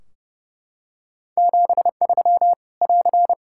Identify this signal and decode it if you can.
Morse code, heard as a keyed tone, about 30 wpm, 700 Hz. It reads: 73 <AR>
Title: Practice QSO - Exchange 4 - Ending - 30wpm